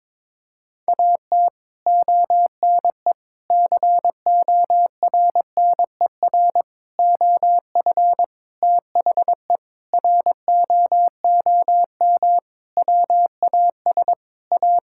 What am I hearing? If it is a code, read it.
Morse code, heard as a keyed tone, about 22 wpm, 700 Hz. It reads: AT ONE CORNER OF THE ROOM WAS A
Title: Sentences from Top 1600 Words 22wpm